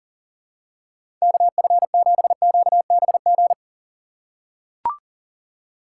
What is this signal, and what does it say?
Morse code, heard as a keyed tone, about 40 wpm, 700 Hz. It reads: KF7QBG
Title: USA Call Signs - 6 Characters - 4x Character Spacing - 40wpm